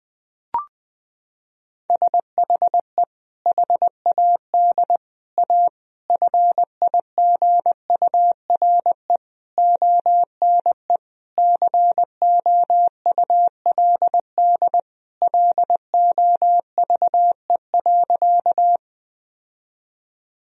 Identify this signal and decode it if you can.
Morse code, heard as a keyed tone, about 20 wpm, 700 Hz. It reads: SHE HAD A FIGURE ONE COULD LOVE.
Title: Sentences from Top 1000 Words - Review Part 2 - 20wpm